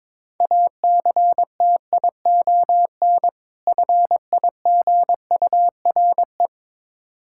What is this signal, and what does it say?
Morse code, heard as a keyed tone, about 22 wpm, 700 Hz. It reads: ACTION FIGURE
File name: Difficult Words with Hints - 22wpm